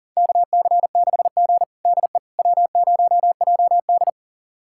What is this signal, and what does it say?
Morse code, heard as a keyed tone, about 40 wpm, 700 Hz. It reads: KC6G DE W0JD
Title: USA Callsign DE Callsign - 4 Characters - 40wpm